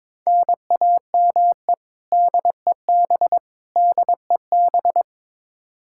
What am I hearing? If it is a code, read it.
Morse code, heard as a keyed tone, about 22 wpm, 700 Hz. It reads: NAME DEB DEB
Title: Practice QSO - Exchange 1 - Beginning - 22wpm